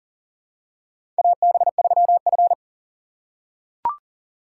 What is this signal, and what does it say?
Morse code, heard as a keyed tone, about 38 wpm, 700 Hz. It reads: AB3F
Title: USA Call Signs - 4 Characters - 2x Character Spacing - 40wpm